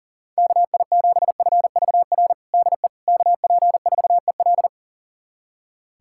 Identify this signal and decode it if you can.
Morse code, heard as a keyed tone, about 40 wpm, 700 Hz. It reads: KI7FVR DE KP4EL
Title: USA Callsign DE Callsign 40wpm